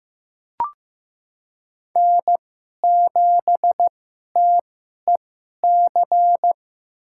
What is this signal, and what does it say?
Morse code, heard as a keyed tone, about 15 wpm, spaced as for 11 wpm, 700 Hz. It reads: N7TEC
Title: USA Call Signs - 5 Characters - 2x Character Spacing - 15wpm